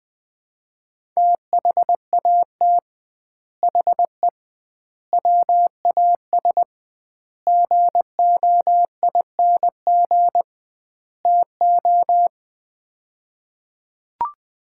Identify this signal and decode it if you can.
Morse code, heard as a keyed tone, about 20 wpm, 700 Hz. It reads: THAT HE WAS GOING TO
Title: 2x Word-Spacing - Sets of 5 Words - 20wpm